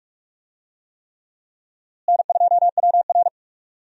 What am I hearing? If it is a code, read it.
Morse code, heard as a keyed tone, about 46 wpm, 700 Hz. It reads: N2WR
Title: USA Call Signs - 4 Characters - Repeated 3x - 45wpm